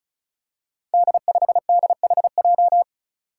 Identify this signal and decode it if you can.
Morse code, heard as a keyed tone, about 36 wpm, 700 Hz. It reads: D5DHJ
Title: Five Challenging Letter-Number Combinations 35wpm